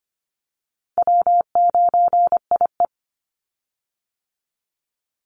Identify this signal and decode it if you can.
Morse code, heard as a keyed tone, about 24 wpm, 700 Hz. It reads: W9IE
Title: Easy Call Signs Repeated 3x - 25wpm